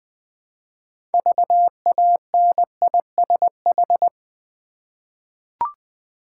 Morse code, heard as a keyed tone, about 20 wpm, 700 Hz. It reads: VANISH
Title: 6 Letter Words - First 3 Letters Given - 20wpm